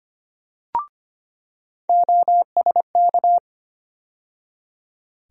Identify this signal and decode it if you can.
Morse code, heard as a keyed tone, about 25 wpm, 700 Hz. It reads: OSK
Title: USA Call Sign Postfixes - 3 Letters 25wpm